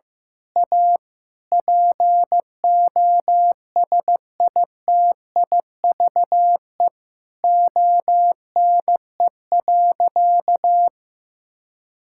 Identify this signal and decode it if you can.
Morse code, heard as a keyed tone, about 15 wpm, 700 Hz. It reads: A POSITIVE ONE.
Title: Sentences from Top 1000 Words - Review Part 6 - 15wpm